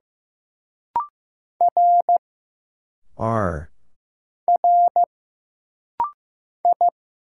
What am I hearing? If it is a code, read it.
Morse code, heard as a keyed tone, about 15 wpm, 700 Hz. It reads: R R I
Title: Single Letters - Rapid-Fire 15wpm